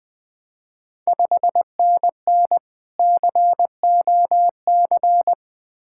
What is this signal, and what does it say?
Morse code, heard as a keyed tone, about 21 wpm, 700 Hz. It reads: 5NN COC
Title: Ontario QSO Party Exchange 20wpm